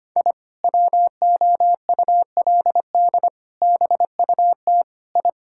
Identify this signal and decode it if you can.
Morse code, heard as a keyed tone, about 25 wpm, 700 Hz. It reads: I WOULD BUT I
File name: practice / 25wpm / Sentences with 2-Word Phrases - 25wpm